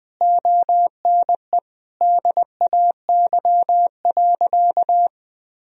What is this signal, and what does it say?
Morse code, heard as a keyed tone, about 20 wpm, 700 Hz. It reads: ONE DAY.
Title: Sentences from Top 500 Words and Abbreviations 20wpm